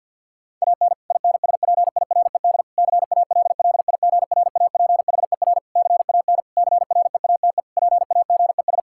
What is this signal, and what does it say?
Morse code, heard as a keyed tone, about 52 wpm, 700 Hz. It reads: AN INSPIRED CALLIGRAPHER CAN CREATE PAGES
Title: Pangram Challenge v2 - 50wpm